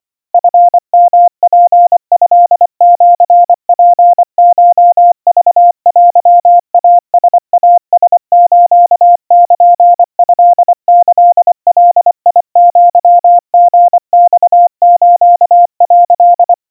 Morse code (Japanese Wabun code), heard as a keyed tone, about 25 wpm, 700 Hz. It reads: チヨツトシツコクテイライラスルトキガアリマス」
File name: 90 cpm 700 Hz Crispy